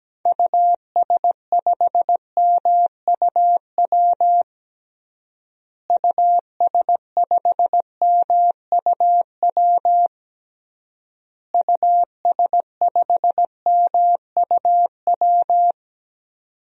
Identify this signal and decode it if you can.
Morse code, heard as a keyed tone, about 17 wpm, 700 Hz. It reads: US5MUW US5MUW US5MUW
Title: DXCC-100 Call Signs - 6 Characters - Repeated 3x - 17wpm